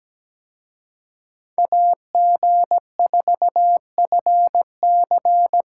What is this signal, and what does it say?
Morse code, heard as a keyed tone, about 20 wpm, 700 Hz. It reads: AG4FC
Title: Speed Racing - USA 2x2 Call Signs 17wpm